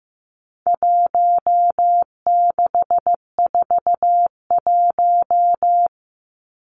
Morse code, heard as a keyed tone, about 15 wpm, 700 Hz. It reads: 1641
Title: Numbers - Contest Style - 4-Digit 15wpm